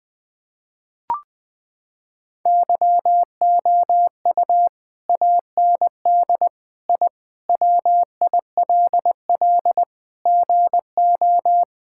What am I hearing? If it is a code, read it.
Morse code, heard as a keyed tone, about 20 wpm, 700 Hz. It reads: YOU AND I WILL GO
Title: Sentences with 2-Word Phrases - 20wpm